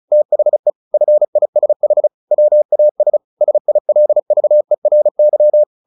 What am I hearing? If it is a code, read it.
Morse code, heard as a keyed tone, about 34 wpm, 600 Hz. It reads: THE FISH WAS SILVERY
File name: the old man and the sea 2 35wpm